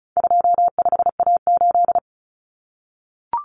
Morse code, heard as a keyed tone, about 34 wpm, 700 Hz. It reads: 25A8
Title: Common Cut Numbers - 4-Digit 35wpm